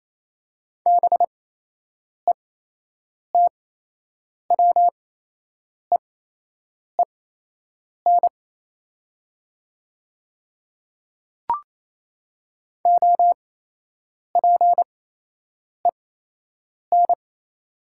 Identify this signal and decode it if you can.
Morse code, heard as a keyed tone, about 28 wpm, spaced as for 8 wpm, 700 Hz. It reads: BETWEEN OPEN
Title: 8x Character Spacing - Top 500 Words - 28wpm